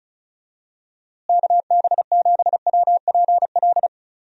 Morse code, heard as a keyed tone, about 34 wpm, 700 Hz. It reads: KB7WPL
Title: USA Call Signs - 6 Characters - Repeated 2x - 35wpm